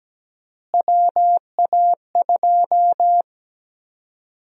Morse code, heard as a keyed tone, about 17 wpm, 700 Hz. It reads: WA2
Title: USA Call Sign Prefixes - Repeated 2x - 17wpm